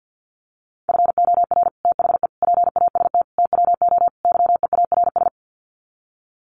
Morse code, heard as a keyed tone, about 50 wpm, 700 Hz. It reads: FOR THE PAST TWO YEARS
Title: Sets of 5 Words Encore2 50wpm